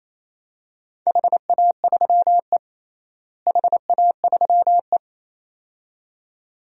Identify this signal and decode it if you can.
Morse code, heard as a keyed tone, about 28 wpm, 700 Hz. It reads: HA3E HA3E
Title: DXCC-100 Call Signs - 4 Characters - Repeated 2x - 28wpm